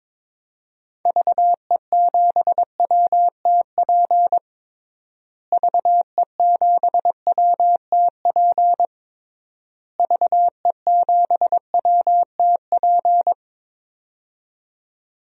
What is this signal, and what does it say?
Morse code, heard as a keyed tone, about 22 wpm, 700 Hz. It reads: VE7WTP VE7WTP VE7WTP
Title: DXCC-100 Call Signs - 6 Characters - Repeated 3x - 22wpm